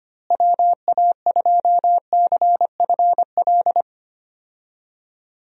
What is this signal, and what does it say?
Morse code, heard as a keyed tone, about 25 wpm, 700 Hz. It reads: WA2CFL
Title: USA Call Signs - 6 Characters - Repeated 3x - 25wpm